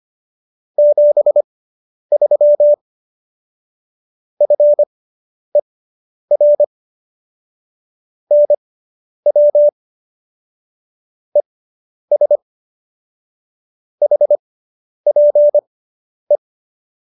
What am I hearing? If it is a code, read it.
Morse code, heard as a keyed tone, about 25 wpm, spaced as for 10 wpm, 590 Hz. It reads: 73 FER NW ES HPE